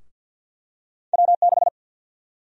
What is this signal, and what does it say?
Morse code, heard as a keyed tone, about 51 wpm, 700 Hz. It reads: W6
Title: USA Call Sign Prefixes - 50wpm